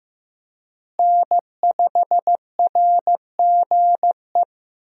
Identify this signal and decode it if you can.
Morse code, heard as a keyed tone, about 15 wpm, 700 Hz. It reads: N5RGE
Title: USA Call Signs - 5 Characters - Repeated 2x - 15wpm